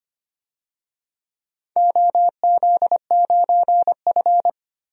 Morse code, heard as a keyed tone, about 25 wpm, 700 Hz. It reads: OZ9F
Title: DXCC-100 Call Signs - 4 Characters - 25wpm